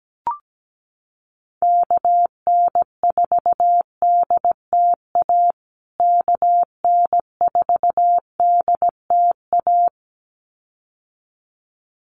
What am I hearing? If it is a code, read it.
Morse code, heard as a keyed tone, about 17 wpm, 700 Hz. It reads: KN4DTA KN4DTA
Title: USA Call Signs Repeated 2x 17wpm